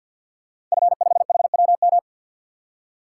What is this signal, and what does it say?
Morse code, heard as a keyed tone, about 52 wpm, 700 Hz. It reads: F5HWM
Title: Code Groups - 5 Letter:Numbers - 50wpm